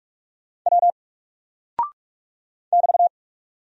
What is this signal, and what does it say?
Morse code, heard as a keyed tone, about 44 wpm, 700 Hz. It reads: W <BT>
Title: Single Character (Letter:Number:Punctuation:Prosign) 45wpm